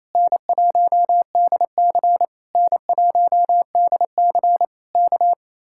Morse code, heard as a keyed tone, about 26 wpm, 700 Hz. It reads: N1DC N1DC K
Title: Speed-Racing Common QSO Phrases - 28wpm